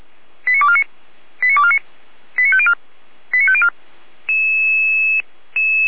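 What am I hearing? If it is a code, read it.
Five-tone selective call (ZVEI), heard as quick runs of five tones, tones of about 70 ms. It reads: 89249 89249 89504 89504
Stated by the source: ZVEI five-tone selective call received by SDR, served by Feuerwehr Orlamünde.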